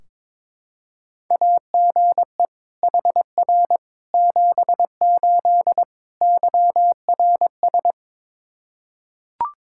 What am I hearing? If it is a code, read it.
Morse code, heard as a keyed tone, about 21 wpm, 700 Hz. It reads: AGE HR 78 YRS
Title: Practice QSO - Exchange 3 - More Info - 22wpm